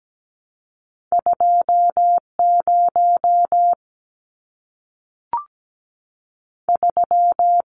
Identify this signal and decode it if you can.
Morse code, heard as a keyed tone, about 17 wpm, 700 Hz. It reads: 20 3
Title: Numbers - 2-Digit 17wpm